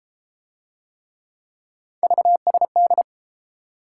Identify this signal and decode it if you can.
Morse code, heard as a keyed tone, about 30 wpm, 700 Hz. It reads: VSD